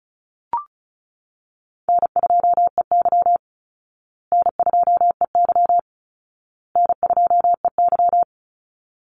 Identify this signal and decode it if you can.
Morse code, heard as a keyed tone, about 34 wpm, 700 Hz. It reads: N2EY N2EY N2EY
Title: Call Signs Repeated 2x - 35wpm